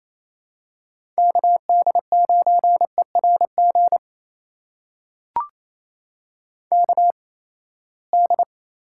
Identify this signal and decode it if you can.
Morse code, heard as a keyed tone, about 28 wpm, spaced as for 19 wpm, 700 Hz. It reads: KD9ERG KD
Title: USA Call Signs - 6 Characters - 8x Character Spacing - 28wpm